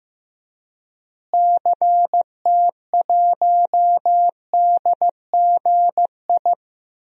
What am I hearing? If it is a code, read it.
Morse code, heard as a keyed tone, about 15 wpm, 700 Hz. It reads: CT1DGI
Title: DXCC-100 Call Signs - 6 Characters - Repeated 2x - 15wpm